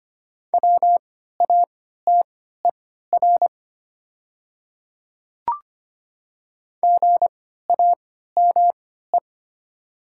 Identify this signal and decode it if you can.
Morse code, heard as a keyed tone, about 25 wpm, spaced as for 14 wpm, 700 Hz. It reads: WATER GAME
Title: 3x Character Spacing - Top 500 Words - 25wpm